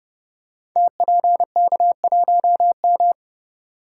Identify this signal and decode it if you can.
Morse code, heard as a keyed tone, about 30 wpm, 700 Hz. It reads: TPK1M
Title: Code Groups - 5 Letter:Numbers - 30wpm